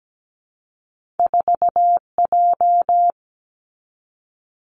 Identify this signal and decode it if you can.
Morse code, heard as a keyed tone, about 17 wpm, 700 Hz. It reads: 4J
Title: Two Challenging Letter-Number Combinations 17wpm